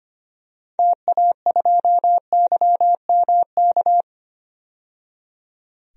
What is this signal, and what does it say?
Morse code, heard as a keyed tone, about 25 wpm, 700 Hz. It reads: TA2YMK
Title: DXCC-100 Call Signs - 6 Characters - 25wpm